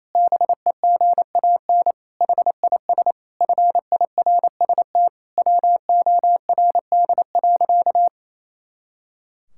Morse code, heard as a keyed tone, about 28 wpm, 700 Hz. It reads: BEGAN HIS FIRST WORD.
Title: Pangram Challenge v2 - 28wpm